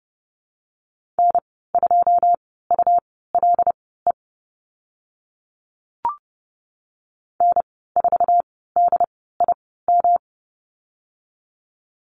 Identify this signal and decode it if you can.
Morse code, heard as a keyed tone, about 30 wpm, spaced as for 17 wpm, 700 Hz. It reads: N2ULE N4DIM
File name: Lesson 19g - Letter w - Call Signs 30wpm